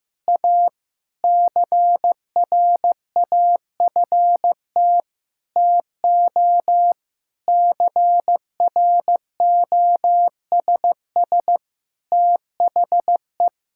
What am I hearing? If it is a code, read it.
Morse code, heard as a keyed tone, about 15 wpm, 700 Hz. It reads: A CRAFT TO CROSS THE
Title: Sentences from Top 1000 Words - Review Part 6 - 15wpm